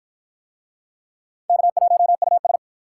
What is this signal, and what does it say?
Morse code, heard as a keyed tone, about 53 wpm, 700 Hz. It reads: K1US